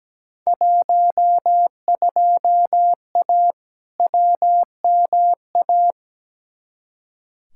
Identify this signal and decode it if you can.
Morse code, heard as a keyed tone, about 17 wpm, 700 Hz. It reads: 12A WMA
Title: Field Day Exchanges Encore - 17wpm